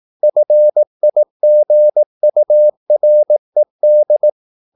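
Morse code, heard as a keyed tone, about 18 wpm, 600 Hz. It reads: FIGURED
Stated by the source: Morse practice at 18 wpm, English text, from CW med Gnister.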